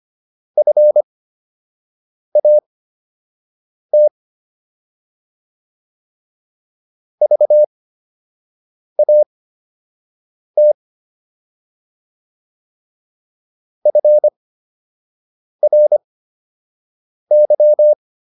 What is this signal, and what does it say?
Morse code, heard as a keyed tone, about 25 wpm, spaced as for 6 wpm, 600 Hz. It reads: FAT VAT FRY